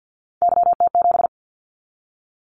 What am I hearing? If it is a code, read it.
Morse code, heard as a keyed tone, about 50 wpm, 700 Hz. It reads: YT7
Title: Callsign Prefixes for Top 25 DXCC - 50wpm